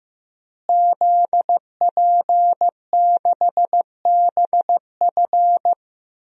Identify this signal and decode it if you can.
Morse code, heard as a keyed tone, about 15 wpm, 700 Hz. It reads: ZP6BF
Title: DXCC-100 Call Signs - 5 Characters - Repeated 3x - 15wpm